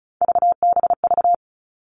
Morse code, heard as a keyed tone, about 34 wpm, 700 Hz. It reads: VBV
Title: All Cut Numbers - 3-Digit 35wpm